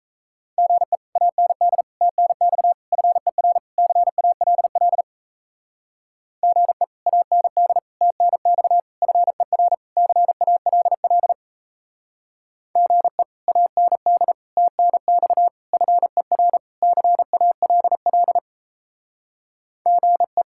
Morse code, heard as a keyed tone, about 36 wpm, 700 Hz. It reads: GE AND TNX FER CALL GE AND TNX FER CALL GE AND TNX FER CALL GE